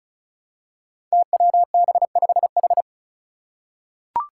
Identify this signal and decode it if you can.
Morse code, heard as a keyed tone, about 36 wpm, 700 Hz. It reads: TWB5H
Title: Five Challenging Letter-Number Combinations 35wpm